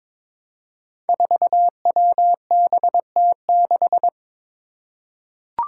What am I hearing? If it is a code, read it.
Morse code, heard as a keyed tone, about 22 wpm, 700 Hz. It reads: 4WBT6
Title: Five Challenging Letter-Number Combinations 22wpm